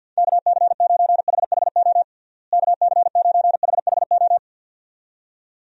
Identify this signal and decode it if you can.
Morse code, heard as a keyed tone, about 51 wpm, 700 Hz. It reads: KC9HHO KC9HHO
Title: USA Call Signs - 6 Characters - Repeated 2x - 50wpm